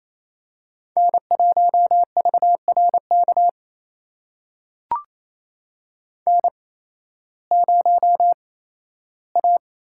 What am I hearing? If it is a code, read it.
Morse code, heard as a keyed tone, about 27 wpm, spaced as for 15 wpm, 700 Hz. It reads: N1VRK N0A
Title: USA Call Signs - 5 Characters - 8x Character Spacing - 28wpm